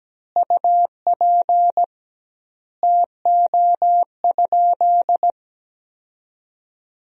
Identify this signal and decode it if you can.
Morse code, heard as a keyed tone, about 17 wpm, 700 Hz. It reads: UP TO?
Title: 2x Word-Spacing - Sentences from Top 100 Words - 17wpm